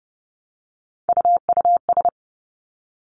Morse code, heard as a keyed tone, about 30 wpm, 700 Hz. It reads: UUS
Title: Three Challenging Letter-Number Combinations 30wpm